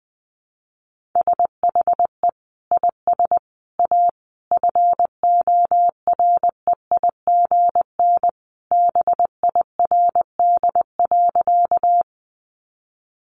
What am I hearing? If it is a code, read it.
Morse code, heard as a keyed tone, about 20 wpm, 700 Hz. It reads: SHE IS A FOREIGN BIRD.
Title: Sentences from Top 1400 Words 20wpm